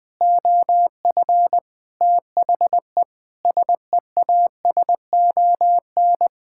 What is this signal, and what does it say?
Morse code, heard as a keyed tone, about 20 wpm, 700 Hz. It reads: OF THE SEASON
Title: Sets of 6 Words - 20wpm